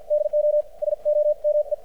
Morse code, heard as a keyed tone, about 53 wpm, spaced as for 45 wpm, 600 Hz. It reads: DO SOME
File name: archived / CRSnet